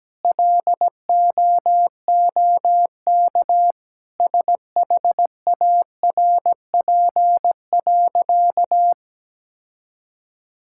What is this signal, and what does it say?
Morse code, heard as a keyed tone, about 17 wpm, 700 Hz. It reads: LOOK SHARP.
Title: Sentences from Top 1600 Words 17wpm